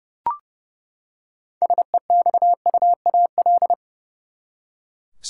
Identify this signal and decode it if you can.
Morse code, heard as a keyed tone, about 29 wpm, 700 Hz. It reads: SEXUAL
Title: Six Letter Words - 30wpm